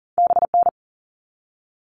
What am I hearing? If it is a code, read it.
Morse code, heard as a keyed tone, about 40 wpm, 700 Hz. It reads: BN
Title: All Cut Numbers - 2-Digit 40wpm